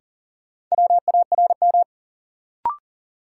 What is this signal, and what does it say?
Morse code, heard as a keyed tone, about 40 wpm, 700 Hz. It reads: WARM
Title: Sentences from Top 500 Words - New Word Drill - 40wpm